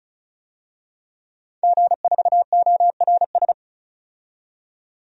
Morse code, heard as a keyed tone, about 34 wpm, 700 Hz. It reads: G4ORS
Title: DXCC-100 Call Signs - 5 Characters - 35wpm